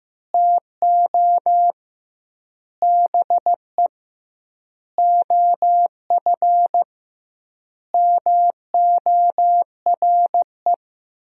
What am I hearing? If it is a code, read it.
Morse code, heard as a keyed tone, about 16 wpm, 700 Hz. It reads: TO BE OF MORE
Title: 2x Word-Spacing - Sentences from Top 400 Words - 15wpm